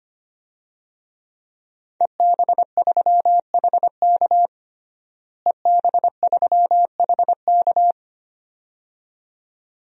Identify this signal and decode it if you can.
Morse code, heard as a keyed tone, about 25 wpm, 700 Hz. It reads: EB3HK EB3HK
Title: DXCC-100 Call Signs - 5 Characters - Repeated 2x - 25wpm